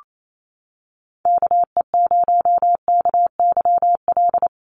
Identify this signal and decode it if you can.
Morse code, heard as a keyed tone, about 28 wpm, 700 Hz. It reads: KE0KYL
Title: USA Call Signs Repeated 3x 28wpm